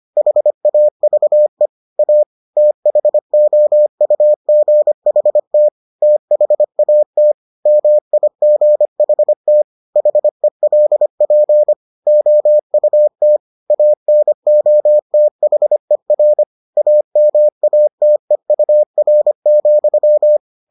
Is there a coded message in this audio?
Yes: Morse code from 0.2 to 20.4 s, heard as a keyed tone, about 25 wpm, 600 Hz. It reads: HAVE A THOUGHT THAT MIGHT HELP OUT ANOTHER AMATEUR,